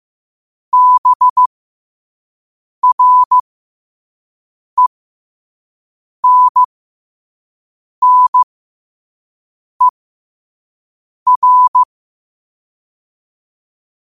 Morse code, heard as a keyed tone, about 15 wpm, spaced as for 5 wpm, 1000 Hz. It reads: BRENNER
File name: qotdfiles / 5wpm